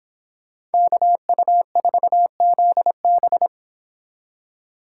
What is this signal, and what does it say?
Morse code, heard as a keyed tone, about 29 wpm, 700 Hz. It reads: KU4ZB